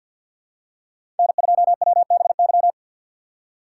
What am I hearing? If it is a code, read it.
Morse code, heard as a keyed tone, about 46 wpm, 700 Hz. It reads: N2WBY
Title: USA Call Signs - 5 Characters - 3x Character Spacing - 50wpm